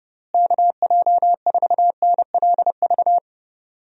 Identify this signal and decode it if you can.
Morse code, heard as a keyed tone, about 30 wpm, 700 Hz. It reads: KJ4NLV
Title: USA Call Signs - 6 Characters - Repeated 2x - 30wpm